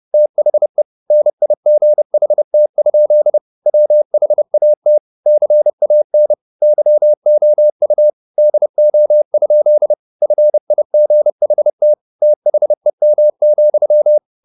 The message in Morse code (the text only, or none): THE NIGHT? WHAT CAN YOU DO? FIGHT THEM,